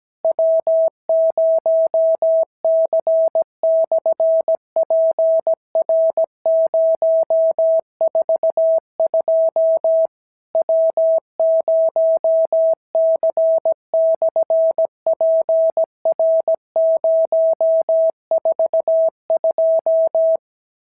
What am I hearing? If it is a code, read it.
Morse code, heard as a keyed tone, about 16 wpm, 650 Hz. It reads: W0C/PR042 W0C/PR042
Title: SOTA QSO Practice 17wpm